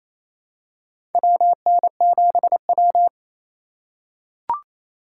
Morse code, heard as a keyed tone, about 28 wpm, 700 Hz. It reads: WN7W
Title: USA Call Signs - 4 Characters - 4x Character Spacing - 28wpm